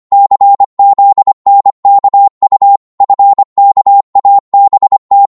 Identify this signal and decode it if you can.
Morse code, heard as a keyed tone, about 25 wpm, 800 Hz. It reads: CZNKU FKABT
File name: Morse / ltr-25wpm